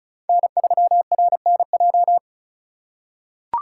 Morse code, heard as a keyed tone, about 36 wpm, 700 Hz. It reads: N3RNJ
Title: USA Call Signs - 5 Characters - 2x Character Spacing - 35wpm